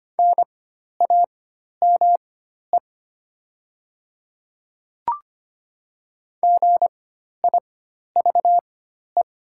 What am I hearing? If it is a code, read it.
Morse code, heard as a keyed tone, about 25 wpm, spaced as for 12 wpm, 700 Hz. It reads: NAME GIVE
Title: 4x Character Spacing - Top 200 Words - 25wpm